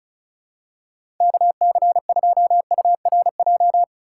Morse code, heard as a keyed tone, about 36 wpm, 690 Hz. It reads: KC2URJ
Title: USA Call Signs - 3x Character Spacing - 35wpm